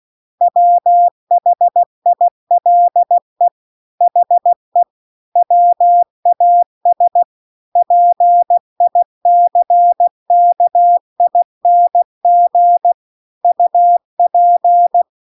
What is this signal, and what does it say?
Morse code, heard as a keyed tone, about 16 wpm, 700 Hz. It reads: WHILE HE WAS PICKING UP
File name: CW-The-Big-Twoheaded-River-2-16WPM